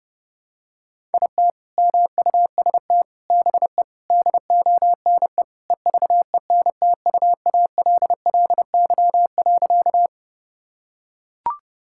Morse code, heard as a keyed tone, about 30 wpm, 700 Hz. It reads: IT MUST BE DONE EVENTUALLY.
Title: Sentences from Top 1000 Words - Review Part 6 - 30wpm